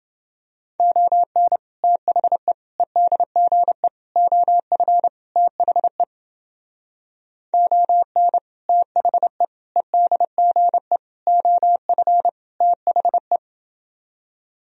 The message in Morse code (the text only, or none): ON THE EDGE OF THE ON THE EDGE OF THE